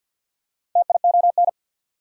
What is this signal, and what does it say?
Morse code, heard as a keyed tone, about 48 wpm, 700 Hz. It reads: TION
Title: Prefixes and Suffixes 50wpm